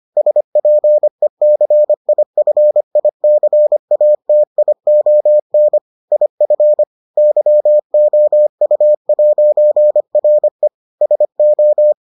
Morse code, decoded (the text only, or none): SPECIFICATION IF YOU'RE SO